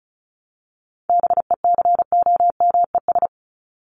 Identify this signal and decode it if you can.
Morse code, heard as a keyed tone, about 34 wpm, 700 Hz. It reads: BECOMES
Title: ES:S Suffix - Top 100 Words with Base Words 35wpm